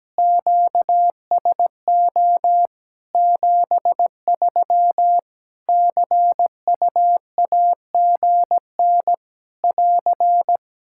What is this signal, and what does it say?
Morse code, heard as a keyed tone, about 17 wpm, 700 Hz. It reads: QSO 73 CUAGN <AR>
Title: Practice QSO - Exchange 4 - Ending - 17wpm